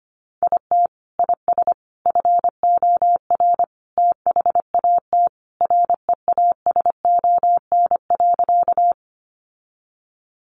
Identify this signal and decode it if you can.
Morse code, heard as a keyed tone, about 25 wpm, 700 Hz. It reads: IT IS FOR THAT REASON.
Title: Sentences from Top 700 Words 25wpm